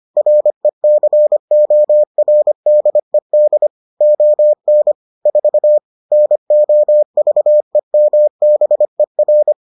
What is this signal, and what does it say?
Morse code, heard as a keyed tone, about 25 wpm, 600 Hz. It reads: RECORDED ON 4 NOVEMBER